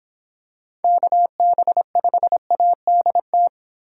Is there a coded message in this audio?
Yes: Morse code from 0.8 to 3.5 s, heard as a keyed tone, about 26 wpm, 700 Hz. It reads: KB5ADT